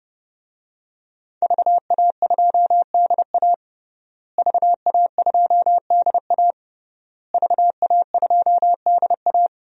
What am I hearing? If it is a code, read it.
Morse code, heard as a keyed tone, about 30 wpm, 700 Hz. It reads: VA2DA VA2DA VA2DA
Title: DXCC-100 Call Signs - 5 Characters - Repeated 3x - 30wpm